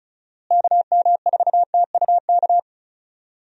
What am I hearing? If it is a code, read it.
Morse code, heard as a keyed tone, about 34 wpm, 700 Hz. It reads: KM4TUK
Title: USA Call Signs - 6 Characters - Repeated 3x - 35wpm